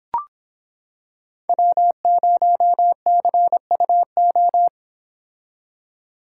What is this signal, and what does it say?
Morse code, heard as a keyed tone, about 25 wpm, 700 Hz. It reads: W0CUO